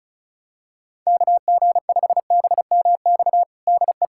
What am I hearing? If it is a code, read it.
Morse code, heard as a keyed tone, about 36 wpm, 700 Hz. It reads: KG5BMX DE
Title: USA Callsign DE Callsign - 6 Characters - 35wpm